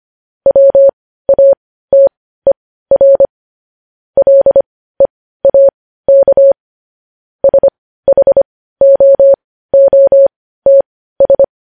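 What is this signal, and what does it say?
Morse code, heard as a keyed tone, about 25 wpm, spaced as for 15 wpm, 550 Hz. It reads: WATER LEAK SHOOTS
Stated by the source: Morse code news headlines sent at 15, opening with vvv.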